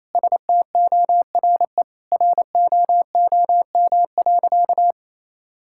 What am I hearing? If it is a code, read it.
Morse code, heard as a keyed tone, about 28 wpm, 700 Hz. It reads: STORE ROOM.
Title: Sentences from Top 1000 Words - Review Part 5 - 28wpm